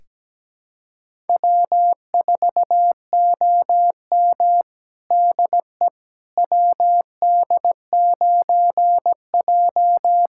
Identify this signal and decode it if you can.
Morse code, heard as a keyed tone, about 17 wpm, 700 Hz. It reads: W4OM DE WD9J
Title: USA Callsign DE Callsign - 4 Characters - 17wpm